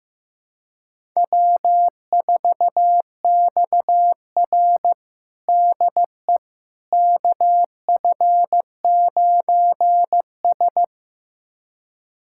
Morse code, heard as a keyed tone, about 15 wpm, 700 Hz. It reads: W4XR DE KF9S
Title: USA Callsign DE Callsign - 4 Characters - 15wpm